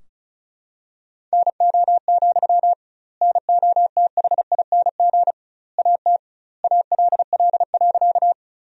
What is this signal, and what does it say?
Morse code, heard as a keyed tone, about 34 wpm, 700 Hz. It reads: NO, NOTHING AT ALL.
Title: 2x Word-Spacing - Sentences from Top 500 Words - 35wpm